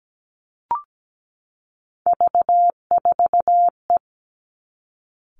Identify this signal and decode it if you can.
Morse code, heard as a keyed tone, about 17 wpm, 700 Hz. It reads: V4E
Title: Three Challenging Letter-Number Combinations 17wpm